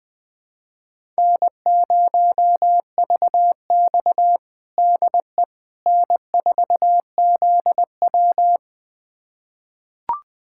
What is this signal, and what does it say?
Morse code, heard as a keyed tone, about 21 wpm, 700 Hz. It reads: N0VX DE N4ZW
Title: USA Callsign DE Callsign - 4 Characters - 20wpm